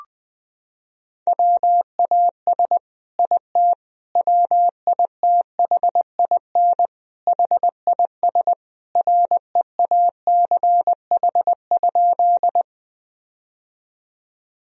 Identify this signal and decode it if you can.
Morse code, heard as a keyed tone, about 20 wpm, 700 Hz. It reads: WAS IT WITHIN HIS REACH?
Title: Sentences from Top 1000 Words - Review Part 4 - 20wpm